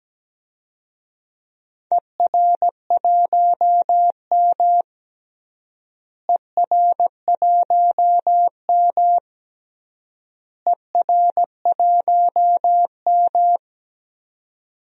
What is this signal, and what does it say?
Morse code, heard as a keyed tone, about 17 wpm, 700 Hz. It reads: ER1M ER1M ER1M
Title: DXCC-100 Call Signs - 4 Characters - Repeated 3x - 17wpm